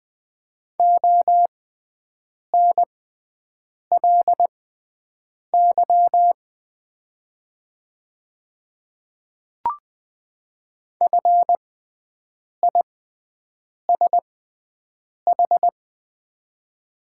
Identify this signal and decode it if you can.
Morse code, heard as a keyed tone, about 20 wpm, spaced as for 7 wpm, 700 Hz. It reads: ONLY FISH
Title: 6x Character Spacing - Top 500 Words - 20wpm